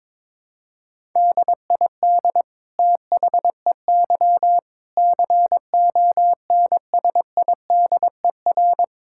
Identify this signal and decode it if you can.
Morse code, heard as a keyed tone, about 22 wpm, 700 Hz. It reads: DID THEY CONSIDER
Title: Sentences from Top 1000 Words - Review Part 6 - 22wpm